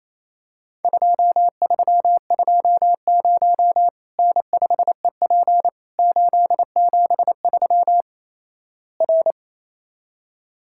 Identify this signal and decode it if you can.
Morse code, heard as a keyed tone, about 28 wpm, 700 Hz. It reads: 2320 N5EP 873 R
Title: Worked All Europe DX Contest - QTC Exchange - 28wpm